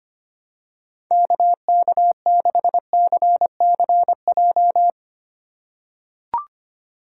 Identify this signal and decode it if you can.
Morse code, heard as a keyed tone, about 25 wpm, 700 Hz. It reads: KK6CCJ
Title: USA Call Signs - 6 Characters - 4x Character Spacing - 25wpm